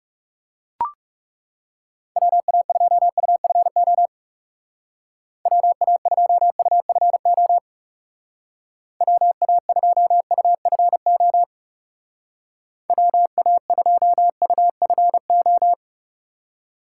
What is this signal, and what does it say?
Morse code, heard as a keyed tone, about 36 wpm, 700 Hz. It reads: WA2UFO WA2UFO WA2UFO WA2UFO
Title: Speed Racing - USA 2x3 Call Signs 30wpm